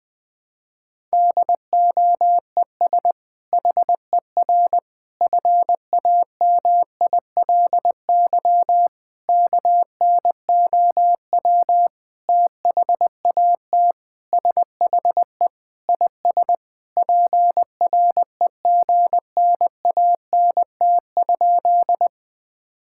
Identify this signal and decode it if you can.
Morse code, heard as a keyed tone, about 20 wpm, 700 Hz. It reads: DOES HER FAMILY KNOW THAT SHE IS PREGNANT?
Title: Sentences from Top 1000 Words - Review Part 6 - 20wpm